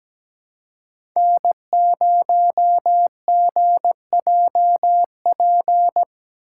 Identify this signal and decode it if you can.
Morse code, heard as a keyed tone, about 17 wpm, 700 Hz. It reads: N0GJP
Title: USA Call Signs - 5 Characters - 4x Character Spacing - 17wpm